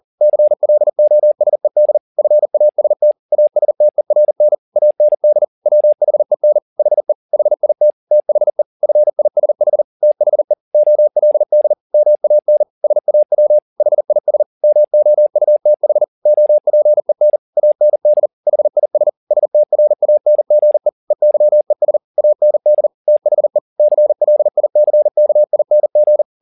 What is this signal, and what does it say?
Morse code, heard as a keyed tone, about 40 wpm, 600 Hz. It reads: CLOSED FAST ASTERN AND WHEN HE HIT THE FISH THE OLD MAN SAW HIS MOUTH OPEN AND HIS STRANGE EYES AND THE CLICKING
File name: the old man and the sea 2 40wpm